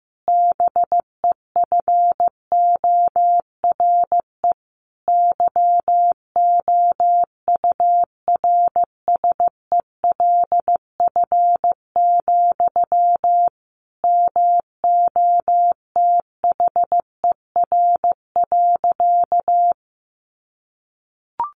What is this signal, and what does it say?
Morse code, heard as a keyed tone, about 15 wpm, 700 Hz. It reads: BEFORE YOURSELF, MOTHER.
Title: Sentences from Top 1300 Words 15wpm